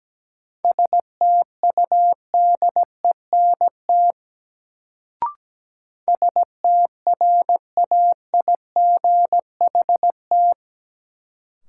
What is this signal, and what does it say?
Morse code, heard as a keyed tone, about 17 wpm, 700 Hz. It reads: STUDENT STRAIGHT
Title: Sentences from Top 1000 Words - New Word Drill - 17wpm